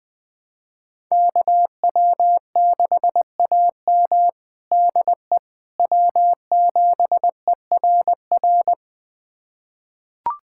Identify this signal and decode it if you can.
Morse code, heard as a keyed tone, about 20 wpm, 700 Hz. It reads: KW6AM DE W7ERR
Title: USA Callsign DE Callsign - 5 Characters - 20wpm